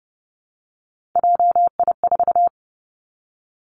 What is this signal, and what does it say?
Morse code, heard as a keyed tone, about 30 wpm, 700 Hz. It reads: JI4
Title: Three Challenging Letter-Number Combinations 30wpm